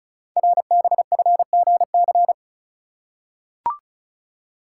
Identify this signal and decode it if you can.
Morse code, heard as a keyed tone, about 36 wpm, 690 Hz. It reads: RBFGC
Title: Code Groups - 5 Letters - 35wpm